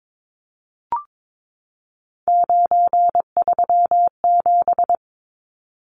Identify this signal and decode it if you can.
Morse code, heard as a keyed tone, about 22 wpm, 700 Hz. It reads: 937
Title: Numbers - 3-Digit 22wpm